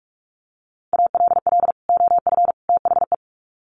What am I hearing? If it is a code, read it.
Morse code, heard as a keyed tone, about 44 wpm, 700 Hz. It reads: ALL OF THE
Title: Sets of 3 Words 45wpm